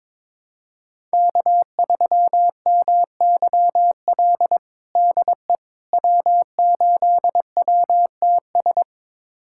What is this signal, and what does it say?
Morse code, heard as a keyed tone, about 22 wpm, 700 Hz. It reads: K3MYL DE W8WTS
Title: USA Callsign DE Callsign - 5 Characters - 22wpm